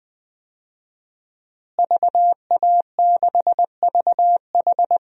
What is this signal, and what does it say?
Morse code, heard as a keyed tone, about 20 wpm, 700 Hz. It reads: VA6VH
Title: DXCC-100 Call Signs - 5 Characters - Repeated 3x - 20wpm